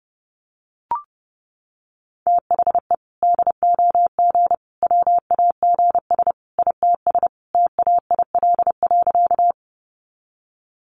Morse code, heard as a keyed tone, about 30 wpm, 700 Hz. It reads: THE DOG WAGS ITS TAIL.
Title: Easy Phrases 30wpm